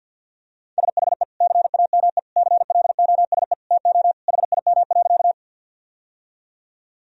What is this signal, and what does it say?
Morse code, heard as a keyed tone, about 48 wpm, 700 Hz. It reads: SHE CAME CLOSE TO HIM.
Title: Sentences from Top 1000 Words - Review Part 1 - 50wpm